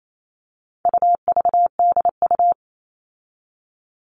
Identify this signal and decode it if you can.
Morse code, heard as a keyed tone, about 28 wpm, 700 Hz. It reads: UVDU
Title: All Cut Numbers - 4-Digit 28wpm